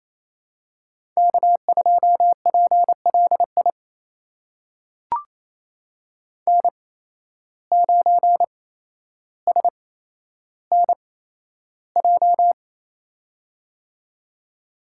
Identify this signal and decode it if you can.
Morse code, heard as a keyed tone, about 28 wpm, spaced as for 12 wpm, 700 Hz. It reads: K2PLI N9SNJ
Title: USA Call Signs - 5 Characters - 8x Character Spacing - 28wpm